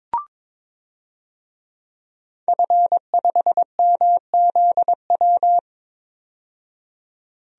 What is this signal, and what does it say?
Morse code, heard as a keyed tone, about 22 wpm, 700 Hz. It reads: F5MZW
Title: DXCC-100 Call Signs - 5 Characters - 22wpm